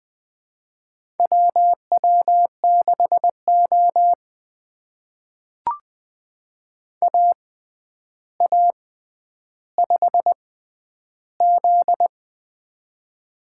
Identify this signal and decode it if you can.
Morse code, heard as a keyed tone, about 20 wpm, spaced as for 10 wpm, 700 Hz. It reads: WW6O AA5Z
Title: USA Call Signs - 4 Characters - 6x Character Spacing - 20wpm